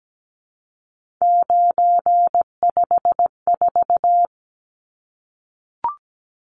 Morse code, heard as a keyed tone, about 17 wpm, 700 Hz. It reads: 954
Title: Numbers - 3-Digit 17wpm